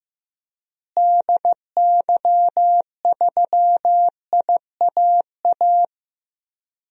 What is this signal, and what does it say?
Morse code, heard as a keyed tone, about 15 wpm, 700 Hz. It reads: DY3IAA
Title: DXCC-100 Call Signs - 6 Characters - Repeated 3x - 15wpm